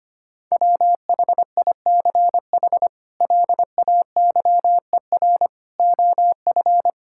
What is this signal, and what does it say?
Morse code, heard as a keyed tone, about 25 wpm, 700 Hz. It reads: WHICH LAYER OF
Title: Sentences from Top 1600 Words 25wpm